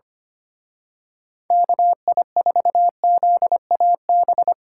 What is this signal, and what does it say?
Morse code, heard as a keyed tone, about 28 wpm, 700 Hz. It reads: KI4ZAB
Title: Speed Racing - USA 2x3 Call Signs 25wpm